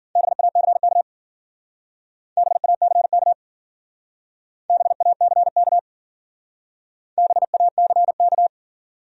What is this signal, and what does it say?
Morse code, heard as a keyed tone, about 52 wpm, 700 Hz. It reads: BACK BACK BACK BACK